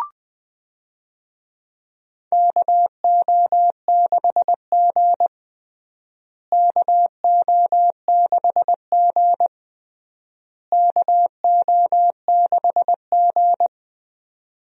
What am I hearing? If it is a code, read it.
Morse code, heard as a keyed tone, about 20 wpm, 700 Hz. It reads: KO6G KO6G KO6G
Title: USA Call Signs - 4 Characters - Repeated 3x - 20wpm